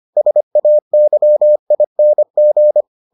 Morse code, heard as a keyed tone, about 25 wpm, 600 Hz. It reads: SAYING